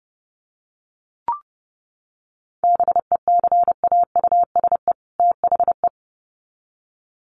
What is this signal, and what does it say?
Morse code, heard as a keyed tone, about 30 wpm, 700 Hz. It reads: BECAUSE THE
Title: Sets of 2 Words 30wpm